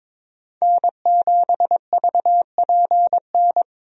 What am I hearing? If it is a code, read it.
Morse code, heard as a keyed tone, about 23 wpm, 700 Hz. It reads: N7VPN
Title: USA Call Signs - 5 Characters - 22wpm